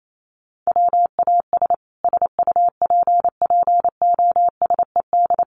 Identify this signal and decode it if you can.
Morse code, heard as a keyed tone, about 28 wpm, 700 Hz. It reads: WAS SUPPOSED
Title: Sets of 4 Words 28wpm